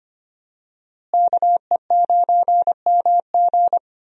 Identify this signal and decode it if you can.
Morse code, heard as a keyed tone, about 25 wpm, 700 Hz. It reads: KE9MG
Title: USA Call Signs - 5 Characters - Repeated 3x - 25wpm